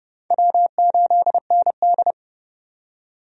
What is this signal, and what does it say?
Morse code, heard as a keyed tone, about 30 wpm, 700 Hz. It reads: W8ND
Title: USA Call Signs - 4 Characters - 30wpm